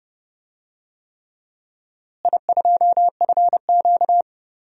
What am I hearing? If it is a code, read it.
Morse code, heard as a keyed tone, about 31 wpm, 700 Hz. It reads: I2FQ
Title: DXCC-100 Call Signs - 4 Characters - Repeated 3x - 30wpm